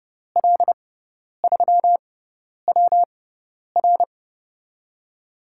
Morse code, heard as a keyed tone, about 30 wpm, spaced as for 10 wpm, 700 Hz. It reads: L3WR
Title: USA Call Signs - 6x Character Spacing - 30wpm